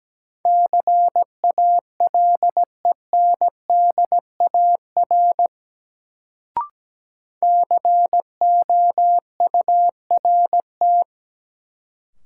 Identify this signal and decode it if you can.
Morse code, heard as a keyed tone, about 17 wpm, 700 Hz. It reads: CALENDAR COURT
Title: Sentences from Top 1600 Words - New Word Drill Alt - 17wpm